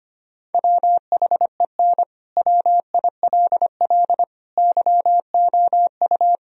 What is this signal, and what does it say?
Morse code, heard as a keyed tone, about 25 wpm, 700 Hz. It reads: WHEN WILL YOU
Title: Sentences from Top 1000 Words - Review Part 3 - 25wpm